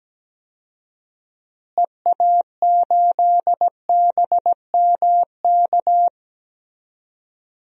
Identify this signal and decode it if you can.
Morse code, heard as a keyed tone, about 17 wpm, 700 Hz. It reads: EA8BMK
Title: DXCC-100 Call Signs - 6 Characters - Repeated 2x - 17wpm